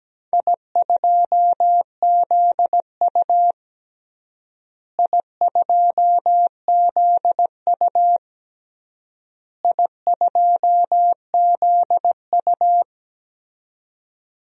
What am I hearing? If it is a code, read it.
Morse code, heard as a keyed tone, about 17 wpm, 700 Hz. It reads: I2ZU I2ZU I2ZU
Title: DXCC-100 Call Signs - 4 Characters - Repeated 3x - 17wpm